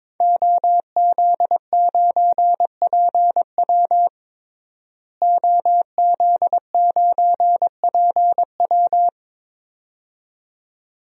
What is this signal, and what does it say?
Morse code, heard as a keyed tone, about 22 wpm, 700 Hz. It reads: OZ9PW OZ9PW
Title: DXCC-100 Call Signs - 5 Characters - Repeated 2x - 22wpm